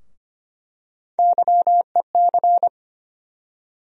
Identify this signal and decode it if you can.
Morse code, heard as a keyed tone, about 25 wpm, 700 Hz. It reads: YEC
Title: USA Call Sign Postfixes - 3 Letters 25wpm